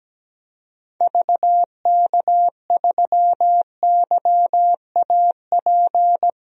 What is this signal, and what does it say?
Morse code, heard as a keyed tone, about 17 wpm, 700 Hz. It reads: VK3YAP
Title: DXCC-100 Call Signs - 6 Characters - 17wpm